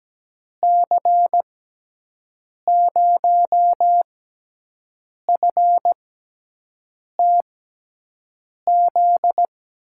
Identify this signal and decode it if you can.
Morse code, heard as a keyed tone, about 17 wpm, spaced as for 6 wpm, 700 Hz. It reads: C0FTZ
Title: USA Call Signs - 6 Characters - 6x Character Spacing - 17wpm